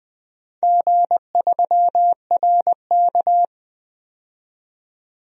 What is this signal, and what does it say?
Morse code, heard as a keyed tone, about 21 wpm, 700 Hz. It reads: G3RK
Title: DXCC-100 Call Signs - 4 Characters - 20wpm